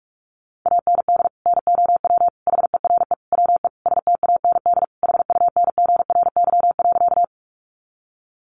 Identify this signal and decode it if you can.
Morse code, heard as a keyed tone, about 44 wpm, 700 Hz. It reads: AND NOW HERE WE STAND HUNGRY.
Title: Sentences from Top 1400 Words 45wpm